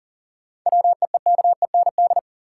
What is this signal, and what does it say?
Morse code, heard as a keyed tone, about 40 wpm, 700 Hz. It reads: WEEKEND
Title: Sets of 5 Words - New Word Drill - 40wpm